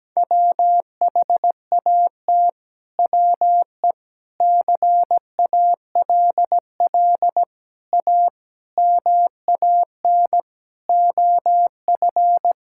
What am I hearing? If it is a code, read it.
Morse code, heard as a keyed tone, about 17 wpm, 700 Hz. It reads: WHAT WE CALL A MAN OF
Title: Sentences from Top 1600 Words 17wpm